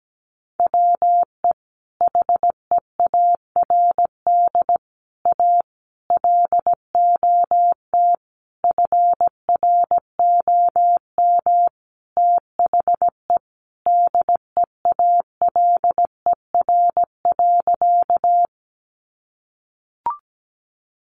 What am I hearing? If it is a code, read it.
Morse code, heard as a keyed tone, about 17 wpm, 700 Hz. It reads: WE HEARD A LOT FROM THE DEALER.
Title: Sentences from Top 1100 Words 17wpm